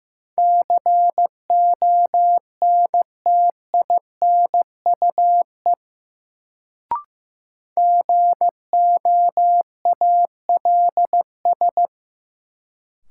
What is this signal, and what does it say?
Morse code, heard as a keyed tone, about 15 wpm, 700 Hz. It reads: CONTINUE GOALS
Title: Sentences from Top 1000 Words - Review Part 6 - New Word Drill - 15wpm